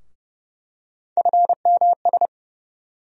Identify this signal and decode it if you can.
Morse code, heard as a keyed tone, about 31 wpm, 700 Hz. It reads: FMS
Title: USA Call Sign Postfixes - 3 Letters 30wpm